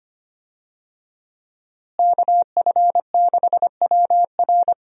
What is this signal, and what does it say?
Morse code, heard as a keyed tone, about 25 wpm, 700 Hz. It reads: KF6WR
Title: USA Call Signs - 5 Characters - Repeated 2x - 25wpm